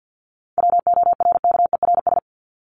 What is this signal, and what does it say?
Morse code, heard as a keyed tone, about 50 wpm, 700 Hz. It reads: WORKERS
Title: ES:S Suffix - Top 100 Words with Base Words 50wpm